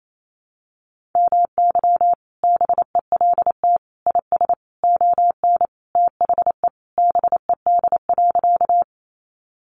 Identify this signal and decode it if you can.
Morse code, heard as a keyed tone, about 28 wpm, 700 Hz. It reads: MY BELT IS ON THE BED.
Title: Sentences from Top 1400 Words 28wpm